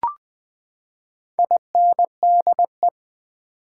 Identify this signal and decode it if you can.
Morse code, heard as a keyed tone, about 20 wpm, 700 Hz. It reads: INDE
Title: Build A Word - 4 Letters At A Time - 12-Letter Words - 20wpm